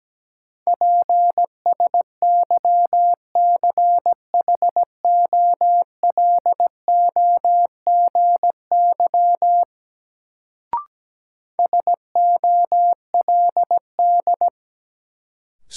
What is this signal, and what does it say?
Morse code, heard as a keyed tone, about 17 wpm, 700 Hz. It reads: PSYCHOLOGY SOLD
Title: Sentences from Top 1400 Words - New Word Drill Alt - 17wpm